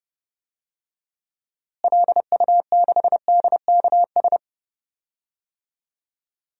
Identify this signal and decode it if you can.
Morse code, heard as a keyed tone, about 32 wpm, 700 Hz. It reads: LU6DKS
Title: DXCC-100 Call Signs - 6 Characters - Repeated 3x - 30wpm